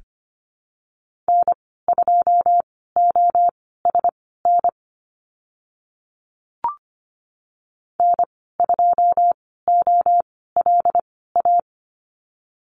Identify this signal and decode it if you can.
Morse code, heard as a keyed tone, about 25 wpm, spaced as for 16 wpm, 700 Hz. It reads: N2OSN N2OLA
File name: Lesson 15g - Number 5 - Call Signs 25wpm